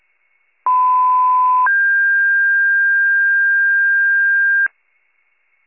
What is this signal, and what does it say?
Two-tone page, heard as a one-second tone, then a three-second tone, about 1010 then 1600 Hz. It is heard from 0.7 to 4.7 s.